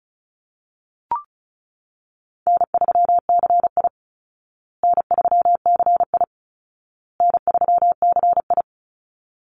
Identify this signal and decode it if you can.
Morse code, heard as a keyed tone, about 34 wpm, 700 Hz. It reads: N3CI N3CI N3CI
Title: Call Signs Repeated 2x - 35wpm